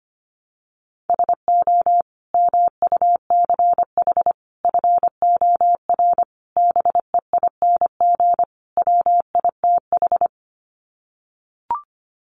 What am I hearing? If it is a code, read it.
Morse code, heard as a keyed tone, about 25 wpm, 700 Hz. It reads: SO MUCH FOR BEING WITH
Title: Sets of 5 Words Encore 25wpm